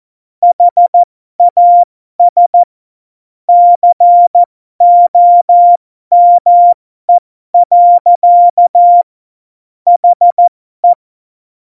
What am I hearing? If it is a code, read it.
Morse code, heard as a keyed tone, about 14 wpm, spaced as for 12 wpm, 700 Hz. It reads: HAS COME. HE
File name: Morsekurs / Witze